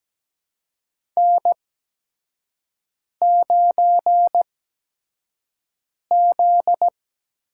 Morse code, heard as a keyed tone, about 17 wpm, spaced as for 5 wpm, 700 Hz. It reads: N9Z
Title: USA Call Signs - 4 Characters - 8x Character Spacing - 17wpm